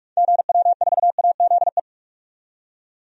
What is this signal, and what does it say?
Morse code, heard as a keyed tone, about 46 wpm, 700 Hz. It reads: GW4AZE
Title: DXCC-100 Call Signs - 6 Characters - 45wpm